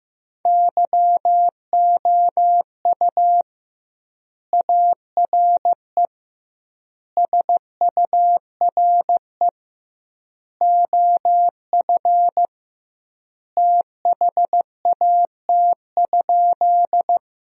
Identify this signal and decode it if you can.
Morse code, heard as a keyed tone, about 15 wpm, 700 Hz. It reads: YOU ARE SURE OF THAT?
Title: 2x Word-Spacing - Sentences from Top 300 Words - 15wpm